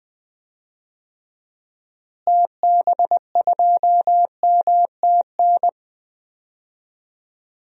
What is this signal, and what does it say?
Morse code, heard as a keyed tone, about 21 wpm, 690 Hz. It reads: TB2MTN
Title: DXCC-100 Call Signs - 6 Characters - Repeated 2x - 20wpm